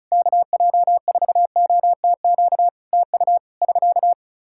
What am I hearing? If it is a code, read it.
Morse code, heard as a keyed tone, about 35 wpm, 700 Hz. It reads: KJ4OTQ TU <SK>
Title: Practice QSO - Exchange 4 - Ending - 35wpm